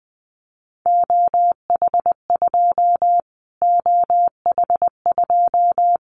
Morse code, heard as a keyed tone, about 20 wpm, 700 Hz. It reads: OH2 OH2
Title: Callsign Prefixes for Top 25 DXCC - Repeated 3x - 20wpm